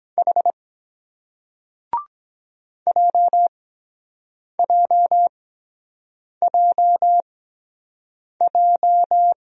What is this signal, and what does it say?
Morse code, heard as a keyed tone, about 23 wpm, spaced as for 7 wpm, 700 Hz. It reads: H JJJJ